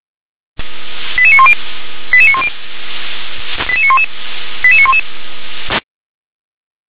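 Five-tone selective call (ZVEI), heard as quick runs of five tones, tones of about 70 ms. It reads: 88910 88910 88911 88911
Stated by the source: ZVEI five-tone selective call received via SDR.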